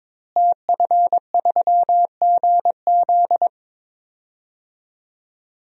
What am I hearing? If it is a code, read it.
Morse code, heard as a keyed tone, about 22 wpm, 700 Hz. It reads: TF3GZ
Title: DXCC-100 Call Signs - 5 Characters - Repeated 3x - 22wpm